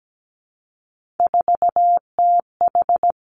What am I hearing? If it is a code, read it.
Morse code, heard as a keyed tone, about 17 wpm, 700 Hz. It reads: 4TH
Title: Three Challenging Letter-Number Combinations 17wpm